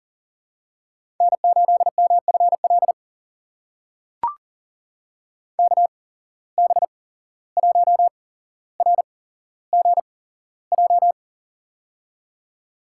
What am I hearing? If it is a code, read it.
Morse code, heard as a keyed tone, about 42 wpm, spaced as for 16 wpm, 700 Hz. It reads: N8MFL KB1RGJ
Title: USA Call Signs - 8x Character Spacing - 40wpm